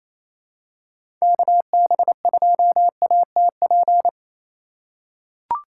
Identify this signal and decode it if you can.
Morse code, heard as a keyed tone, about 28 wpm, 700 Hz. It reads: KB2ATP
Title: USA Call Signs - 6 Characters - 3x Character Spacing - 28wpm